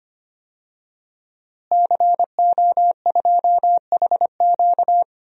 Morse code, heard as a keyed tone, about 25 wpm, 700 Hz. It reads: CO2HQ
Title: DXCC-100 Call Signs - 5 Characters - Repeated 2x - 25wpm